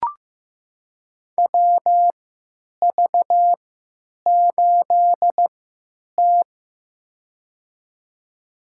Morse code, heard as a keyed tone, about 15 wpm, spaced as for 9 wpm, 700 Hz. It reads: WV8T
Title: USA Call Signs - 4 Characters - 3x Character Spacing - 15wpm